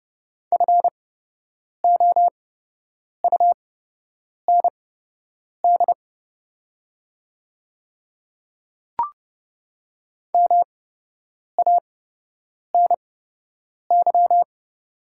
Morse code, heard as a keyed tone, about 30 wpm, spaced as for 8 wpm, 700 Hz. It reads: FOUND MANY
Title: 8x Character Spacing - Top 300 Words - 30wpm